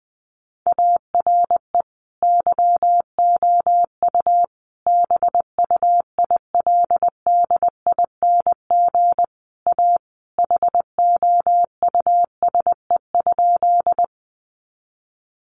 Morse code, heard as a keyed tone, about 20 wpm, 700 Hz. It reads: ARE YOU BUILDING A HOUSE?
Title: Sentences from Top 700 Words 20wpm